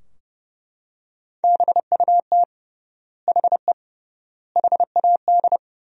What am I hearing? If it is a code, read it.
Morse code, heard as a keyed tone, about 30 wpm, 700 Hz. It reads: BUT HE HAD
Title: 3x Word-Spacing - Sentences from Top 100 Words - 30wpm